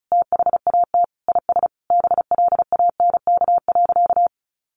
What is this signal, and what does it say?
Morse code, heard as a keyed tone, about 34 wpm, 700 Hz. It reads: THAT IS BLANK.
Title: Sentences from Top 1500 Words 35wpm